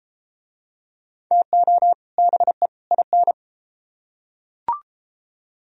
Morse code, heard as a keyed tone, about 30 wpm, 700 Hz. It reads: TO BE IN